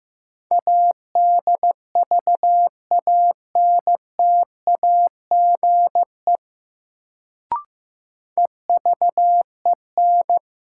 Morse code, heard as a keyed tone, about 15 wpm, 700 Hz. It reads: ADVANTAGE EVEN
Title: Sets of 4 Words - New Word Drill - 15wpm